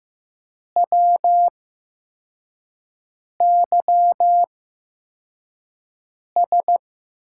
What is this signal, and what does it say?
Morse code, heard as a keyed tone, about 15 wpm, spaced as for 4 wpm, 700 Hz. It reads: WYS
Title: USA Call Signs - 5 Characters - 8x Character Spacing - 15wpm